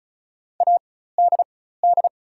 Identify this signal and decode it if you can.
Morse code, heard as a keyed tone, about 36 wpm, spaced as for 16 wpm, 690 Hz. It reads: ADD
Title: 4x Character Spacing - Top 500 Words - 35wpm